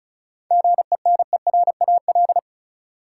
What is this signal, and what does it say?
Morse code, heard as a keyed tone, about 36 wpm, 700 Hz. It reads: GENERAL
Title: Sentences from Top 600 Words - New Word Drill - 35wpm